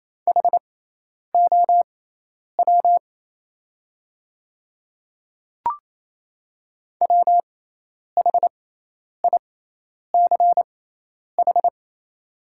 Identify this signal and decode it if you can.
Morse code, heard as a keyed tone, about 28 wpm, spaced as for 10 wpm, 700 Hz. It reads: HOW WHICH
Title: 6x Character Spacing - Top 100 Words - 28wpm